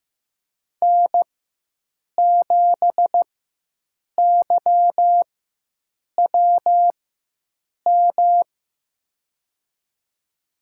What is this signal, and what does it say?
Morse code, heard as a keyed tone, about 15 wpm, spaced as for 7 wpm, 700 Hz. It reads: N7YWM
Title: USA Call Signs - 5 Characters - 4x Character Spacing - 15wpm